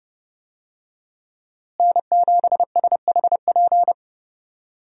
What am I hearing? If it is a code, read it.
Morse code, heard as a keyed tone, about 30 wpm, 700 Hz. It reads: N7SHP
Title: USA Call Signs - 5 Characters - Repeated 2x - 30wpm